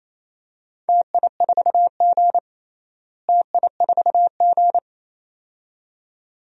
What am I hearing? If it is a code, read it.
Morse code, heard as a keyed tone, about 28 wpm, 700 Hz. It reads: TI4G TI4G
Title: DXCC-100 Call Signs - 4 Characters - Repeated 2x - 28wpm